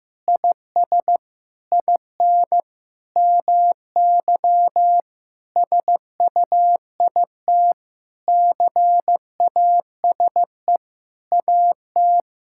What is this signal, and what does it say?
Morse code, heard as a keyed tone, about 15 wpm, 700 Hz. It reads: IS IN MY SUIT CASE AT
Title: Sentences from Top 1600 Words 15wpm